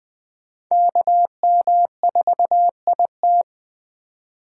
Morse code, heard as a keyed tone, about 20 wpm, 700 Hz. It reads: KM4IT
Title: USA Call Signs - 5 Characters - 4x Character Spacing - 20wpm